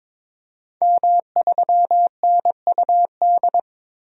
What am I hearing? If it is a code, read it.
Morse code, heard as a keyed tone, about 23 wpm, 700 Hz. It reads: M3NUD
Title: DXCC-100 Call Signs - 5 Characters - 22wpm